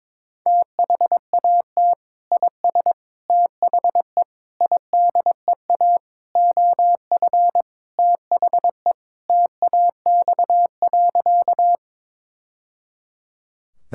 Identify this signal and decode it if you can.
Morse code, heard as a keyed tone, about 22 wpm, 700 Hz. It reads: THAT IS THE IDEA OF THE TAX.
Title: Sentences from Top 1000 Words - Review Part 5 - 22wpm